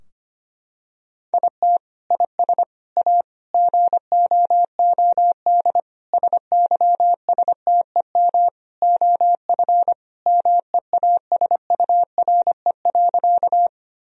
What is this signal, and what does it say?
Morse code, heard as a keyed tone, about 25 wpm, 700 Hz. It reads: IT IS A GOOD SYSTEM OF MEASURE.
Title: Sentences from Top 1000 Words - Review Part 3 - 25wpm